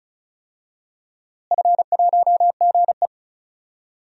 Morse code, heard as a keyed tone, about 34 wpm, 700 Hz. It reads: F1GE
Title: DXCC-100 Call Signs - 4 Characters - Repeated 3x - 35wpm